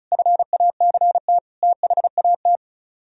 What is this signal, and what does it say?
Morse code, heard as a keyed tone, about 43 wpm, 700 Hz. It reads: FACT THAT
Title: Speed-Racing - Sets of 4 Words 35wpm